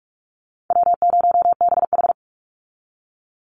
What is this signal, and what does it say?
Morse code, heard as a keyed tone, about 44 wpm, 700 Hz. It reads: W0BH
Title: Easy Call Signs Repeated 2x - 45wpm